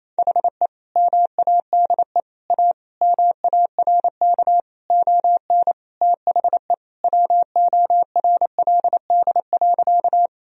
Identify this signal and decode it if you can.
Morse code, heard as a keyed tone, about 28 wpm, 700 Hz. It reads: HE MADE A MARK ON THE WORLD.
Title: Sentences from Top 1000 Words - Review Part 1 - 28wpm